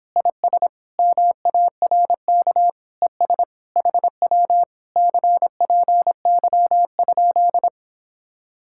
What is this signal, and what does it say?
Morse code, heard as a keyed tone, about 29 wpm, 700 Hz. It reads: IS MARK ES HW CPY?